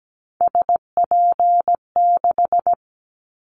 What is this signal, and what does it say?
Morse code, heard as a keyed tone, about 17 wpm, 700 Hz. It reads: SP6
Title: Callsign Prefixes for Top 25 DXCC - Repeated 3x - 17wpm